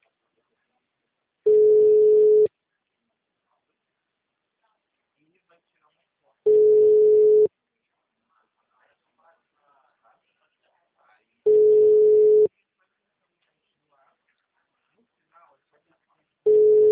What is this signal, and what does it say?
Ringback tone (Europe, 425 Hz), with speech.